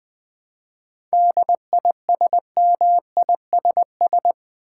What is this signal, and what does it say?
Morse code, heard as a keyed tone, about 20 wpm, 700 Hz. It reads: DISMISS
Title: 7 Letter Words - First 3 Letters Given - 20wpm